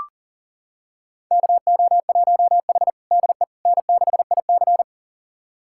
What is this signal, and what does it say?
Morse code, heard as a keyed tone, about 40 wpm, 700 Hz. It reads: KO1H DE N6IC
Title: USA Callsign DE Callsign - 4 Characters - 40wpm